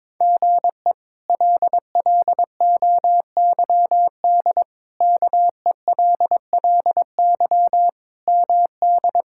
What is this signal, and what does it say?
Morse code, heard as a keyed tone, about 22 wpm, 700 Hz. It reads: GE LLOYD KELLY MD
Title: K1USN SST QSO Practice - 22wpm